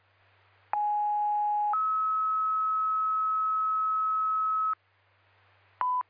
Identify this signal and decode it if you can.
Two-tone page, heard as a one-second tone, then a three-second tone, about 830 then 1290 Hz. It is heard from 0.7 to 4.7 s.